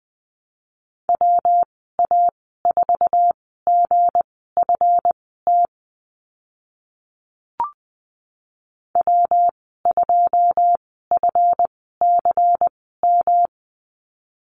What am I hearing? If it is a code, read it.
Morse code, heard as a keyed tone, about 20 wpm, spaced as for 14 wpm, 700 Hz. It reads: WA4GFT W2FCM
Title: 161 Lesson 26g - Letter g - Call Signs 20wpm